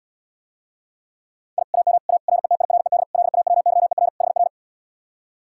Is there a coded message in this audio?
Yes: Morse code from 1.6 to 4.5 s, heard as a keyed tone, about 79 wpm, 700 Hz. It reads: I AM A RETIRED LAWYER HR